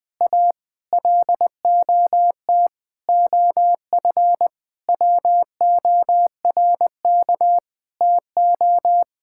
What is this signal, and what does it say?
Morse code, heard as a keyed tone, about 20 wpm, 700 Hz. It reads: A LOT OF WORK TO
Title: Sentences with 9-Word Phrases - 20wpm